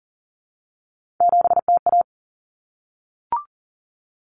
Morse code, heard as a keyed tone, about 40 wpm, 700 Hz. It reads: 7TA
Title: Common Cut Numbers - 3-Digit 40wpm